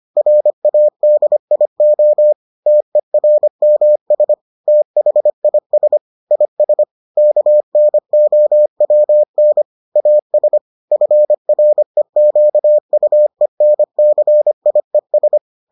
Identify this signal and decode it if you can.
Morse code, heard as a keyed tone, about 25 wpm, 600 Hz. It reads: RADIO TERMS THIS IS KNOWN AS FREQUENCIES